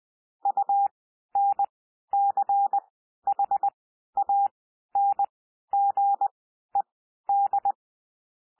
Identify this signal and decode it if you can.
Morse code, heard as a keyed tone, about 20 wpm, spaced as for 12 wpm, 800 Hz. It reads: UNCHANGED
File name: cwmp3 / P1 SOUND PASSAGE 20 wpm